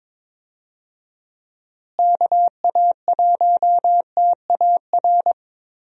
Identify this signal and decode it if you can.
Morse code, heard as a keyed tone, about 22 wpm, 700 Hz. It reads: KA1TAR
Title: USA Call Signs - 6 Characters - Repeated 2x - 22wpm